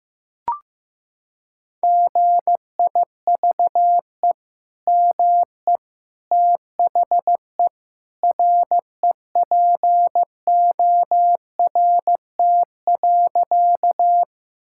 Morse code, heard as a keyed tone, about 14 wpm, 700 Hz. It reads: GIVE ME THE REPORT.
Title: Sentences from Top 1000 Words - Review Part 6 - 15wpm